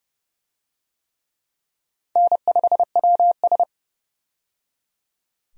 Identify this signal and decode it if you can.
Morse code, heard as a keyed tone, about 30 wpm, 700 Hz. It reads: N5WS
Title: USA Call Signs - 4 Characters - 30wpm